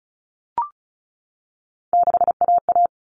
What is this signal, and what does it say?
Morse code, heard as a keyed tone, about 34 wpm, 700 Hz. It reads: 6AA
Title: All Cut Numbers - 3-Digit 35wpm